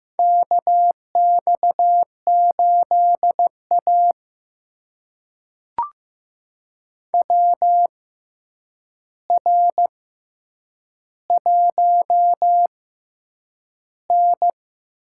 Morse code, heard as a keyed tone, about 15 wpm, spaced as for 8 wpm, 700 Hz. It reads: KX8A WR1N
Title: USA Call Signs - 4 Characters - 6x Character Spacing - 15wpm